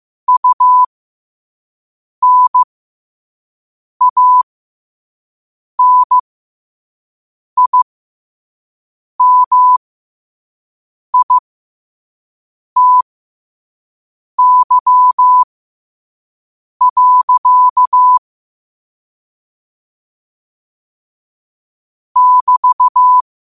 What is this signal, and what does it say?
Morse code, heard as a keyed tone, about 15 wpm, spaced as for 5 wpm, 1000 Hz. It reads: UNANIMITY. <BT>
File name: qotdfiles / 5wpm